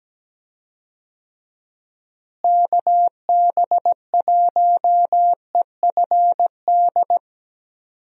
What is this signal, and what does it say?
Morse code, heard as a keyed tone, about 17 wpm, 700 Hz. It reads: KB1EFD
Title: USA Call Signs - 6 Characters - Repeated 2x - 17wpm